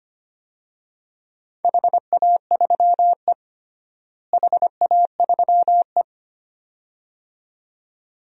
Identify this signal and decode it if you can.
Morse code, heard as a keyed tone, about 25 wpm, 700 Hz. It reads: HA3E HA3E
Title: DXCC-100 Call Signs - 4 Characters - Repeated 2x - 25wpm